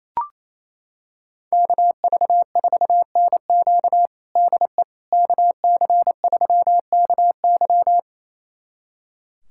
Morse code, heard as a keyed tone, about 27 wpm, 700 Hz. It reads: KV4NQ DE KC3KY
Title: USA Callsign DE Callsign 28wpm